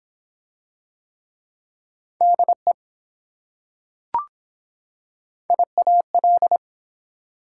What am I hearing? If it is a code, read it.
Morse code, heard as a keyed tone, about 24 wpm, 700 Hz. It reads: DE IAL